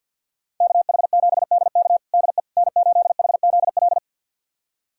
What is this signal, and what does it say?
Morse code, heard as a keyed tone, about 50 wpm, 700 Hz. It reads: KH7DK DE N8HZL
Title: USA Callsign DE Callsign - 5 Characters - 50wpm